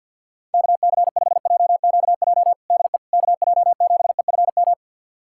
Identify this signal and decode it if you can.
Morse code, heard as a keyed tone, about 50 wpm, 700 Hz. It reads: KC5JQJ DE KJ7EFM
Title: USA Callsign DE Callsign - 6 Characters - 50wpm